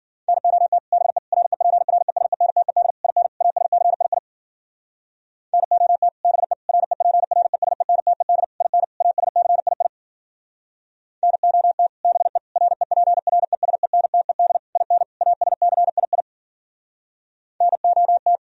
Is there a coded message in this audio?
Yes: Morse code from 0.3 to 18.4 s, heard as a keyed tone, about 52 wpm, 700 Hz. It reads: NOT BE REPRESENTED IN ASCII NOT BE REPRESENTED IN ASCII NOT BE REPRESENTED IN ASCII NOT